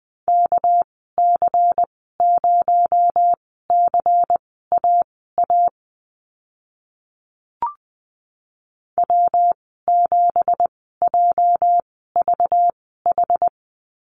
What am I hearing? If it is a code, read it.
Morse code, heard as a keyed tone, about 20 wpm, spaced as for 14 wpm, 700 Hz. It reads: KC0CAA W7JVH
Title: 231 Lesson 37g - Letter x - Call Signs 20wpm